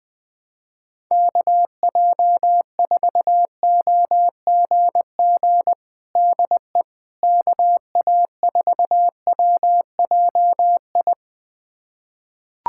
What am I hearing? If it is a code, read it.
Morse code, heard as a keyed tone, about 20 wpm, 700 Hz. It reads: KJ4OGG DE KA4WJI
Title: USA Callsign DE Callsign - 6 Characters - 20wpm